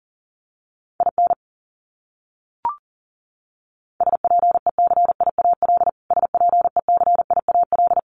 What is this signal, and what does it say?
Morse code, heard as a keyed tone, about 40 wpm, 700 Hz. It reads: IN SPECIAL SPECIAL
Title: Top 500 Words Repeated 40wpm